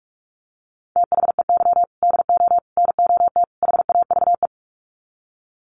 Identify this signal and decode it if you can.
Morse code, heard as a keyed tone, about 44 wpm, 700 Hz. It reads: THEY DO NOT HAVE
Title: Sets of 4 Words 45wpm